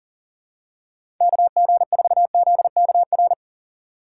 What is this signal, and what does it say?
Morse code, heard as a keyed tone, about 40 wpm, 700 Hz. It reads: KG4ZKR
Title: USA Call Signs - 6x Character Spacing - 40wpm